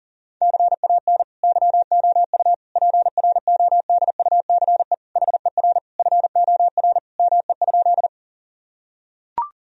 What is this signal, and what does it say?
Morse code, heard as a keyed tone, about 37 wpm, 700 Hz. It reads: CAN YOU PRODUCE HER FOR ME?
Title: Sentences from Top 1000 Words - Review Part 3 - 40wpm